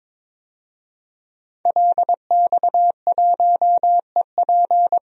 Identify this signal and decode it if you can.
Morse code, heard as a keyed tone, about 21 wpm, 700 Hz. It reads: LX1EP
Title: DXCC-100 Call Signs - 5 Characters - 22wpm